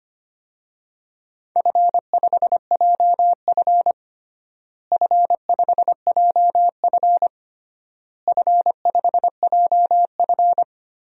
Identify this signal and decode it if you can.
Morse code, heard as a keyed tone, about 25 wpm, 700 Hz. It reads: F5JF F5JF F5JF
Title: DXCC-100 Call Signs - 4 Characters - Repeated 3x - 25wpm